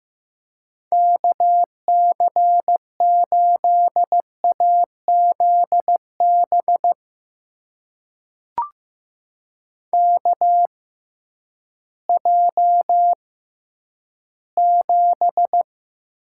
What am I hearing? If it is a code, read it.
Morse code, heard as a keyed tone, about 15 wpm, spaced as for 10 wpm, 700 Hz. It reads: KC8AZB KJ7
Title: USA Call Signs - 6x Character Spacing - 15wpm